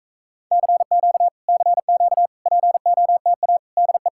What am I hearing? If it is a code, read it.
Morse code, heard as a keyed tone, about 40 wpm, 700 Hz. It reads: CQ CQ POTA DE